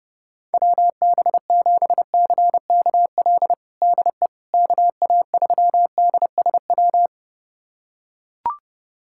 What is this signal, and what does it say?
Morse code, heard as a keyed tone, about 29 wpm, 700 Hz. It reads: WB7CKL DE KA3DSW
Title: USA Callsign DE Callsign - 6 Characters - 30wpm